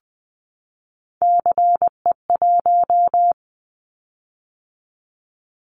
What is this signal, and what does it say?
Morse code, heard as a keyed tone, about 20 wpm, 700 Hz. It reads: CE1
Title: Callsign Prefixes for Top 100 DXCC - 20wpm